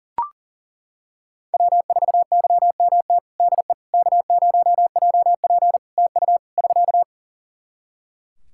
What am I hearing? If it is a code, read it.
Morse code, heard as a keyed tone, about 40 wpm, 700 Hz. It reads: W4YMT DE K0JP TU <SK>
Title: Practice QSO - Exchange 4 - Ending - 40wpm